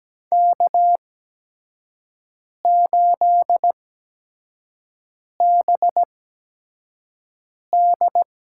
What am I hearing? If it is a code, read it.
Morse code, heard as a keyed tone, about 17 wpm, spaced as for 5 wpm, 700 Hz. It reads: K8BD
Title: USA Call Signs - 8x Character Spacing - 17wpm